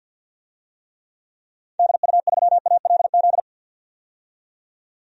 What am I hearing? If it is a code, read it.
Morse code, heard as a keyed tone, about 49 wpm, 700 Hz. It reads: DU3ALZ
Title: DXCC-100 Call Signs - 6 Characters - 50wpm